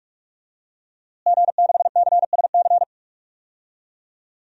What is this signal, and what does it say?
Morse code, heard as a keyed tone, about 43 wpm, 700 Hz. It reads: G6CSC
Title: DXCC-100 Call Signs - 5 Characters - Repeated 3x - 45wpm